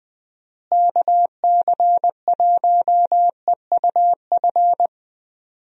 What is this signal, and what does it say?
Morse code, heard as a keyed tone, about 21 wpm, 700 Hz. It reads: KC1EUF
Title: USA Call Signs - 6 Characters - Repeated 2x - 20wpm